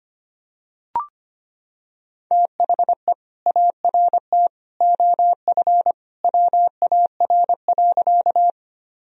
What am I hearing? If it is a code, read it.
Morse code, heard as a keyed tone, about 25 wpm, 700 Hz. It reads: THE ART OF WAR.
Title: Sentences from Top 1000 Words - Review Part 3 - 25wpm